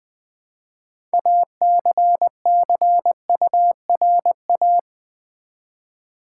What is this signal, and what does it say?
Morse code, heard as a keyed tone, about 20 wpm, 700 Hz. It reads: ACCURA
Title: Build A Word - 3 Letters At A Time - 8-Letter Words - 20wpm